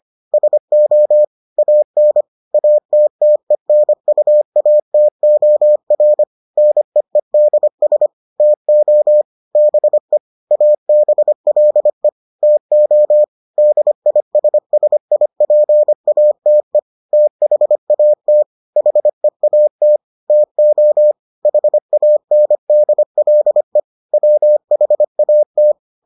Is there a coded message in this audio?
Yes: Morse code from 0.3 to 25.7 s, heard as a keyed tone, about 25 wpm, 600 Hz. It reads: SO AN ATTENUATOR NEEDS TO BE ABLE TO DISSIPATE THAT HEAT TO HANDLE WHAT